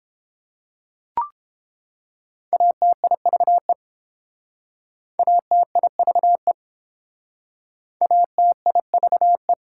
Morse code, heard as a keyed tone, about 30 wpm, 700 Hz. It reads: ATIVE ATIVE ATIVE